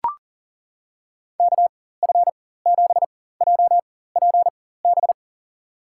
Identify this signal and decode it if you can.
Morse code, heard as a keyed tone, about 40 wpm, spaced as for 19 wpm, 700 Hz. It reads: KF7JPB
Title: USA Call Signs - 6 Characters - 4x Character Spacing - 40wpm